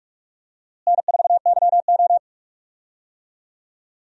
Morse code, heard as a keyed tone, about 46 wpm, 700 Hz. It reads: N4YO
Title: USA Call Signs - 4 Characters - Repeated 3x - 45wpm